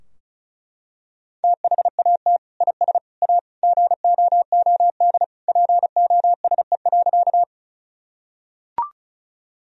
Morse code, heard as a keyed tone, about 36 wpm, 700 Hz. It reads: THAT IS A GOOD POSE.
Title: Sentences from Top 1000 Words - Review Part 2 - 35wpm